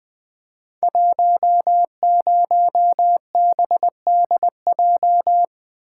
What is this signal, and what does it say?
Morse code, heard as a keyed tone, about 20 wpm, 700 Hz. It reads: 10BDJ
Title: Code Groups - 5 Letter:Numbers - 20wpm